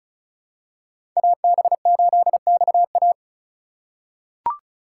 Morse code, heard as a keyed tone, about 34 wpm, 700 Hz. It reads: AB8XA
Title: USA Call Signs - 5 Characters - 2x Character Spacing - 35wpm